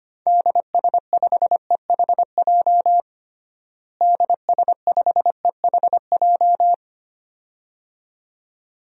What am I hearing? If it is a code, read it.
Morse code, heard as a keyed tone, about 25 wpm, 700 Hz. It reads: DS5EHJ DS5EHJ
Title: DXCC-100 Call Signs - 6 Characters - Repeated 2x - 25wpm